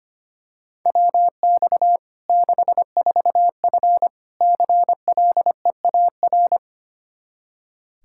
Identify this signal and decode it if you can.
Morse code, heard as a keyed tone, about 25 wpm, 700 Hz. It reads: WX 64F CLEAR
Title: K1USN SST QSO Practice 25wpm